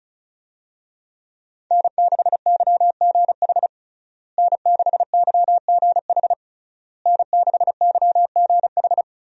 Morse code, heard as a keyed tone, about 36 wpm, 700 Hz. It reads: N6YGH N6YGH N6YGH
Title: USA Call Signs - 5 Characters - Repeated 3x - 35wpm